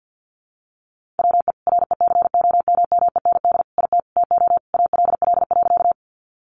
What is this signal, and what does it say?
Morse code, heard as a keyed tone, about 50 wpm, 700 Hz. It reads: WE RECOMMEND IT TO ALL.
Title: Sentences from Top 1200 Words 50wpm